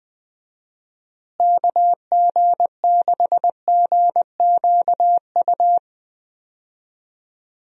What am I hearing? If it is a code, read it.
Morse code, heard as a keyed tone, about 20 wpm, 700 Hz. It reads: KG6GQU
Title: USA Call Signs - 6 Characters - Repeated 3x - 20wpm